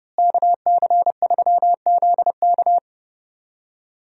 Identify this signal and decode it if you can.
Morse code, heard as a keyed tone, about 30 wpm, 700 Hz. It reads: KC3ZK
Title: USA Call Signs - 5 Characters - Repeated 2x - 30wpm